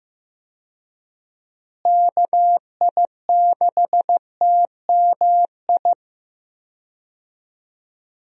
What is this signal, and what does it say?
Morse code, heard as a keyed tone, about 16 wpm, 700 Hz. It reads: KI6TMI
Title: USA Call Signs - 6 Characters - Repeated 3x - 15wpm